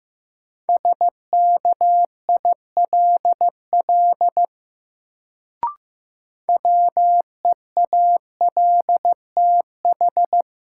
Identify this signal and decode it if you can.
Morse code, heard as a keyed tone, about 15 wpm, 700 Hz. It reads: SKILL WEALTH
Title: Sentences from Top 1200 Words - New Word Drill Alt - 15wpm